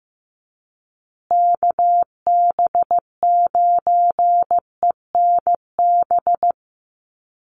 Morse code, heard as a keyed tone, about 15 wpm, 700 Hz. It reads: KB9ENB
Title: USA Call Signs Repeated 3x 15wpm